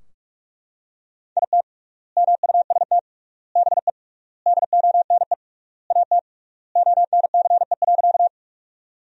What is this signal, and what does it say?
Morse code, heard as a keyed tone, about 43 wpm, 700 Hz. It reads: IT MUST BE DONE AT ONCE.
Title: 3x Word-Spacing - Sentences from Top 500 Words - 45wpm